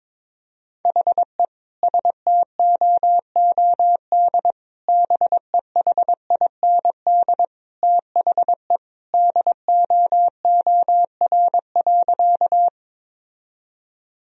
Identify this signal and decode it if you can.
Morse code, heard as a keyed tone, about 22 wpm, 700 Hz. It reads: HE STOOD BEHIND THE DOOR.
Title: Sentences from Top 1000 Words - Review Part 2 - 22wpm